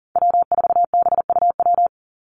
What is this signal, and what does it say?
Morse code, heard as a keyed tone, about 40 wpm, 700 Hz. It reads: W4BUW
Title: Call Signs - 40wpm